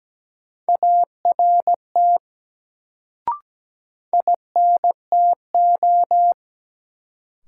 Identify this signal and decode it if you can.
Morse code, heard as a keyed tone, about 17 wpm, 700 Hz. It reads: ART INTO
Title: Sentences from Top 600 Words - New Word Drill - 17wpm